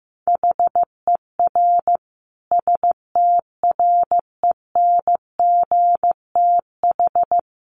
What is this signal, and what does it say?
Morse code, heard as a keyed tone, about 15 wpm, 700 Hz. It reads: HER STRENGTH
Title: Sentences from Top 1300 Words 15wpm